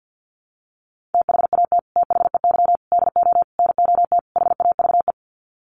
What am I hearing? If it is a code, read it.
Morse code, heard as a keyed tone, about 50 wpm, 700 Hz. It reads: THAT THEY DO NOT HAVE
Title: Sets of 5 Words Encore2 50wpm